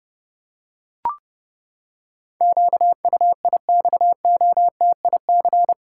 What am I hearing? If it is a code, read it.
Morse code, heard as a keyed tone, about 30 wpm, 700 Hz. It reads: QUIXOTIC
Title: Pangram Challenge v2 - 30wpm